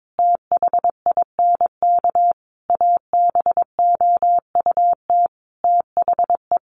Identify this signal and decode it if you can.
Morse code, heard as a keyed tone, about 22 wpm, 700 Hz. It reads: THINK ABOUT THE
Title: Sentences from Top 700 Words 22wpm